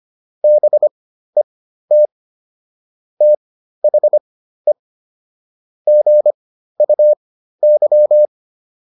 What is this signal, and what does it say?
Morse code, heard as a keyed tone, about 25 wpm, spaced as for 13 wpm, 600 Hz. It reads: BET THE GUY